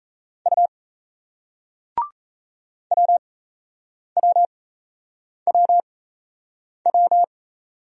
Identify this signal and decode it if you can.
Morse code, heard as a keyed tone, about 39 wpm, spaced as for 8 wpm, 700 Hz. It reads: U WWWW